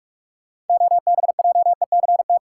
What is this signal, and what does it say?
Morse code, heard as a keyed tone, about 43 wpm, 700 Hz. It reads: OBJECT
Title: Sentences from Top 500 Words - New Word Drill - 45wpm